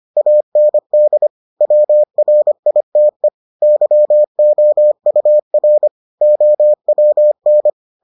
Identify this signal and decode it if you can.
Morse code, heard as a keyed tone, about 25 wpm, 600 Hz. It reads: AND WRITE YOUR OWN